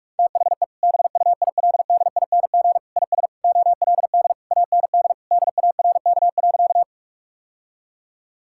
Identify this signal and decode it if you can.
Morse code, heard as a keyed tone, about 46 wpm, 700 Hz. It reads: THE BUILDING IS OLD AND DARK.
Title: Sentences from Top 1000 Words - Review Part 5 - 45wpm